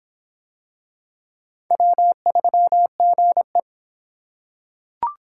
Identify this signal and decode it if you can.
Morse code, heard as a keyed tone, about 24 wpm, 700 Hz. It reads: W3GE